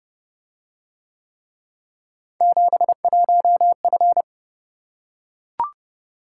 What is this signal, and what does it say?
Morse code, heard as a keyed tone, about 28 wpm, 700 Hz. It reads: 71F